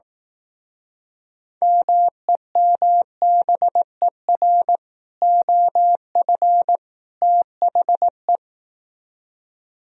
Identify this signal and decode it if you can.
Morse code, heard as a keyed tone, about 18 wpm, 700 Hz. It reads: MEMBER OF THE
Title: Speed-Racing - Sets of 3 Words 15wpm